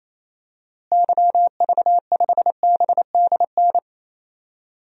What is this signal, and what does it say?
Morse code, heard as a keyed tone, about 28 wpm, 700 Hz. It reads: YV5BDN
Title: DXCC-100 Call Signs - 6 Characters - Repeated 2x - 28wpm